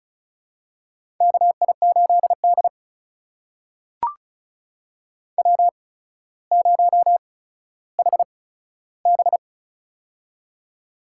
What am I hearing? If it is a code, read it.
Morse code, heard as a keyed tone, about 36 wpm, spaced as for 15 wpm, 700 Hz. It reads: KI8D W0HB
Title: USA Call Signs - 4 Characters - 8x Character Spacing - 35wpm